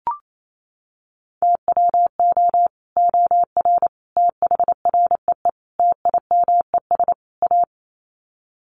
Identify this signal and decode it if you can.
Morse code, heard as a keyed tone, about 28 wpm, 700 Hz. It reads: TWO OR THREE TIMES A
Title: Sets of 5 Words Encore2 28wpm